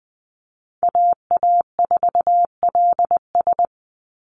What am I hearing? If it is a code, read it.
Morse code, heard as a keyed tone, about 20 wpm, 700 Hz. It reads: AA4LS
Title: Call Signs Repeated 2x - 20wpm